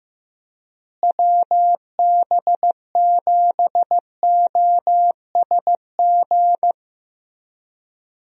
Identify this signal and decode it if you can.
Morse code, heard as a keyed tone, about 15 wpm, 700 Hz. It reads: WB7OSG
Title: USA Call Signs - 6 Characters - Repeated 3x - 15wpm